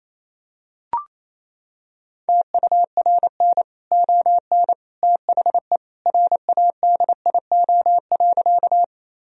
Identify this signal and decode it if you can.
Morse code, heard as a keyed tone, about 28 wpm, 700 Hz. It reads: TURN ON THE RADIO.
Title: Sentences from Top 1000 Words - Review Part 3 - 28wpm